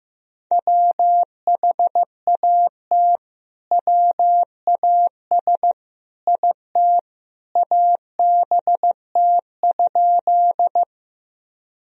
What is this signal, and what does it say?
Morse code, heard as a keyed tone, about 16 wpm, 700 Hz. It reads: WHAT WAS IT ABT?
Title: Sentences from Top 100 Words and Abbreviations 15wpm